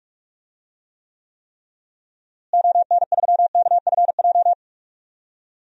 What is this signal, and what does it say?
Morse code, heard as a keyed tone, about 42 wpm, 700 Hz. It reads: ON3KFJ
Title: DXCC-100 Call Signs - 6 Characters - Repeated 3x - 45wpm